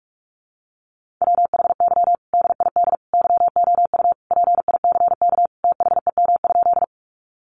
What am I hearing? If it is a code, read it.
Morse code, heard as a keyed tone, about 46 wpm, 700 Hz. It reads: WHY DID YOU PICK THEM?
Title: Sentences from Top 1000 Words 45wpm